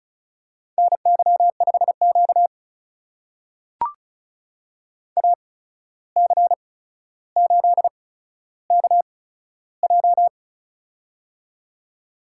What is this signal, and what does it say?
Morse code, heard as a keyed tone, about 36 wpm, spaced as for 14 wpm, 700 Hz. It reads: NY5Q AC8KJ
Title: USA Call Signs - 8x Character Spacing - 35wpm